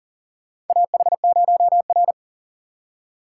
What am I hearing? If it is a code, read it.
Morse code, heard as a keyed tone, about 40 wpm, 700 Hz. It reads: AH0R
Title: USA Call Signs - 4 Characters - 4x Character Spacing - 40wpm